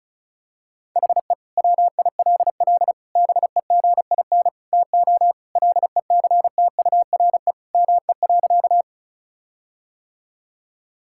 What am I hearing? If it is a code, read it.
Morse code, heard as a keyed tone, about 34 wpm, 700 Hz. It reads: HE WILL BEGIN TO LECTURE ME.
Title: Sentences from Top 1600 Words 35wpm